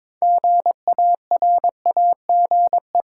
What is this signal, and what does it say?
Morse code, heard as a keyed tone, about 22 wpm, 700 Hz. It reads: GARAGE
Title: Six Letter Words - 22wpm